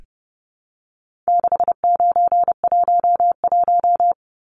Morse code, heard as a keyed tone, about 30 wpm, 700 Hz. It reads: 6911
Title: Numbers - 4-Digit 30wpm